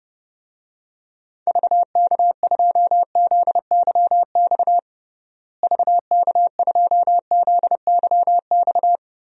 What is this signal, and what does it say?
Morse code, heard as a keyed tone, about 30 wpm, 700 Hz. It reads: VK2ZYX VK2ZYX
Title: DXCC-100 Call Signs - 6 Characters - Repeated 3x - 30wpm